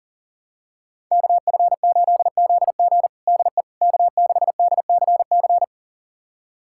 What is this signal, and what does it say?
Morse code, heard as a keyed tone, about 40 wpm, 700 Hz. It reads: KF8ZG DE K6DCC
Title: USA Callsign DE Callsign - 5 Characters - 40wpm